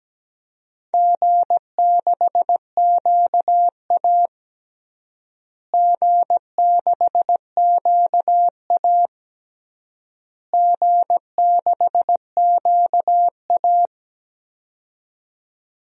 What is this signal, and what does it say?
Morse code, heard as a keyed tone, about 17 wpm, 700 Hz. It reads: G6QA G6QA G6QA
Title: DXCC-100 Call Signs - 4 Characters - Repeated 3x - 17wpm